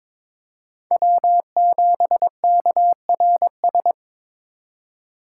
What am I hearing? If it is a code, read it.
Morse code, heard as a keyed tone, about 22 wpm, 700 Hz. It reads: W7KRS
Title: K1USN SST QSO Practice - 22wpm